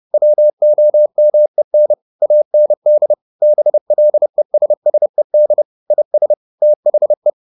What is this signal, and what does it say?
Morse code, heard as a keyed tone, about 30 wpm, 600 Hz. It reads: WOMEN AND BLESSED IS THE
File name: the old man and the sea / the old man and the sea 1 30wpm